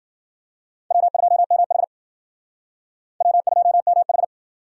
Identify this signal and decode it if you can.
Morse code, heard as a keyed tone, about 54 wpm, 700 Hz. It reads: W2MH W2MH